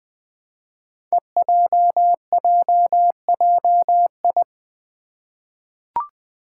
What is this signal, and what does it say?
Morse code, heard as a keyed tone, about 20 wpm, 700 Hz. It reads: EJJJI
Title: Five Challenging Letter-Number Combinations 20wpm